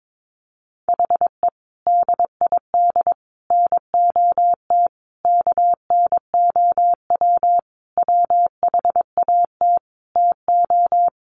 Morse code, heard as a keyed tone, about 22 wpm, 700 Hz. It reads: HE DID NOT KNOW WHAT TO
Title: Sentences from Top 600 Words 22wpm